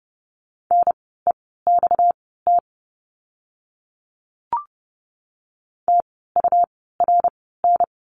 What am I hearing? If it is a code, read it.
Morse code, heard as a keyed tone, about 30 wpm, spaced as for 17 wpm, 700 Hz. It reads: NEXT TURN
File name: course / 30:10wpm / 230 Lesson 37f - Letter x - Words 30wpm